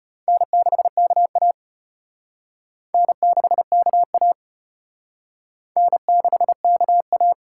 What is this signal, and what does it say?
Morse code, heard as a keyed tone, about 33 wpm, 700 Hz. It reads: N6KA N6KA N6KA